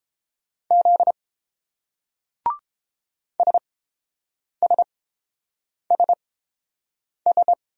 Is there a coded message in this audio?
Yes: Morse code from 0.7 to 7.5 s, heard as a keyed tone, about 29 wpm, spaced as for 7 wpm, 700 Hz. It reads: Z SSSS